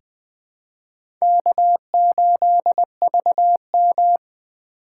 Morse code, heard as a keyed tone, about 20 wpm, 700 Hz. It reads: K8VM
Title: USA Call Signs - 4 Characters - 3x Character Spacing - 20wpm